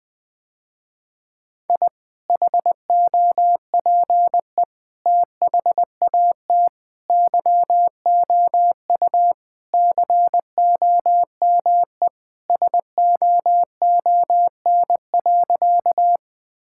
Morse code, heard as a keyed tone, about 21 wpm, 700 Hz. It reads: I HOPE THAT YOU COME SOON.
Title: Sentences with 2-Word Phrases - Spoken Beginning - 20wpm